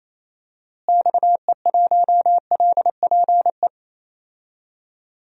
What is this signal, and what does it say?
Morse code, heard as a keyed tone, about 28 wpm, 700 Hz. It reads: XE1LPE
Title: DXCC-100 Call Signs - 6 Characters - Repeated 3x - 28wpm